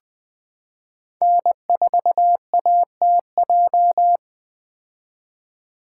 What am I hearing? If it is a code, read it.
Morse code, heard as a keyed tone, about 20 wpm, 700 Hz. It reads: N4ATJ
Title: 2023 POTA Activator Call Signs - 20wpm